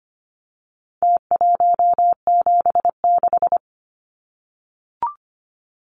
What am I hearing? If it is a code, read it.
Morse code, heard as a keyed tone, about 25 wpm, 700 Hz. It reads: T176
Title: Numbers - Contest Style - 4-Digit 25wpm